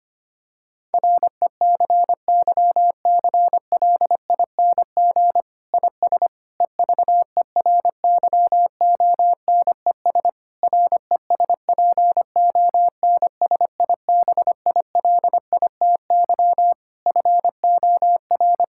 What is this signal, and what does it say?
Morse code, heard as a keyed tone, about 25 wpm, 700 Hz. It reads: RECYCLING IS EVERYONES RESPONSIBILITY FOR
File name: practice / 25wpm / Difficult Sentences with Hints - 25wpm